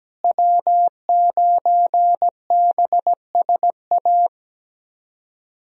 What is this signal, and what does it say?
Morse code, heard as a keyed tone, about 17 wpm, 700 Hz. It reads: W9BSA
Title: USA Call Signs - 5 Characters - Repeated 2x - 17wpm